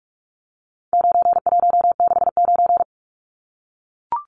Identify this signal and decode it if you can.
Morse code, heard as a keyed tone, about 44 wpm, 700 Hz. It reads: 9169
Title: Numbers - 4-Digit 45wpm